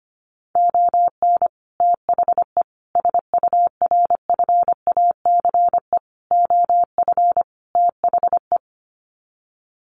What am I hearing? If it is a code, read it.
Morse code, heard as a keyed tone, about 25 wpm, 700 Hz. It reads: ON THE SURFACE OF THE
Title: Sets of 5 Words Encore 25wpm